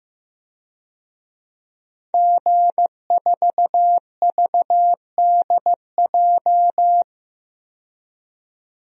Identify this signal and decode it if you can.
Morse code, heard as a keyed tone, about 15 wpm, 700 Hz. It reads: G4VDJ
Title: DXCC-100 Call Signs - 5 Characters - 15wpm